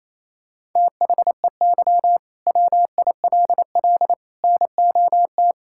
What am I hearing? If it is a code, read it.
Morse code, heard as a keyed tone, about 28 wpm, 700 Hz. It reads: THEY WILL NOT
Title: Sentences from Top 1000 Words - Review Part 5 - 28wpm